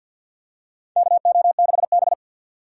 Morse code, heard as a keyed tone, about 46 wpm, 690 Hz. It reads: KO6B
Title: USA Call Signs - 2x Character Spacing - 50wpm